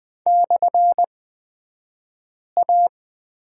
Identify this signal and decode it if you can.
Morse code, heard as a keyed tone, about 20 wpm, spaced as for 5 wpm, 700 Hz. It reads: /A
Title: 249 Lesson 40d - BK - Two Characters 20wpm